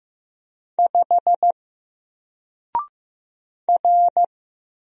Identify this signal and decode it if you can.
Morse code, heard as a keyed tone, about 15 wpm, 700 Hz. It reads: 5 R
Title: Single Letter-Number 15wpm